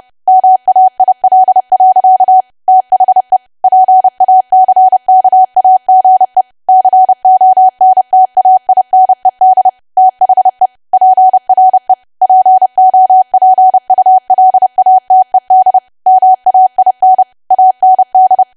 Morse code, heard as a keyed tone, about 30 wpm, 750 Hz. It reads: MAIL. THE PACKAGE CONTAINED THE PRE POPULATED MAIN AND
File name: cwa / W1AW-Level3 / 30WPM